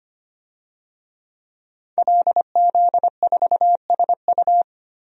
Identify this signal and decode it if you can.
Morse code, heard as a keyed tone, about 25 wpm, 700 Hz. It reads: LZ4SU
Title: DXCC-100 Call Signs - 5 Characters - 25wpm